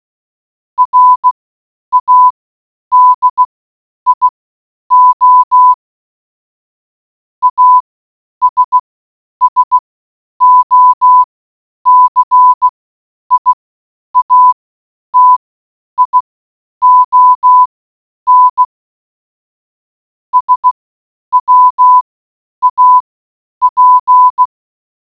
Morse code, heard as a keyed tone, about 16 wpm, spaced as for 9 wpm, 1000 Hz. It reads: RADIO ASSOCIATION SWAP